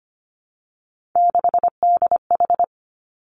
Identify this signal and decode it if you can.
Morse code, heard as a keyed tone, about 25 wpm, 700 Hz. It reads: 6DH
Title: Three Challenging Letter-Number Combinations 25wpm